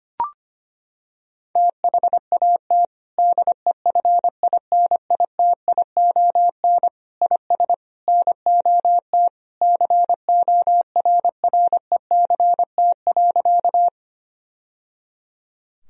Morse code, heard as a keyed tone, about 25 wpm, 700 Hz. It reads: THAT DEFINITION IS NOT CORRECT.
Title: Sentences from Top 1000 Words - Review Part 4 - 25wpm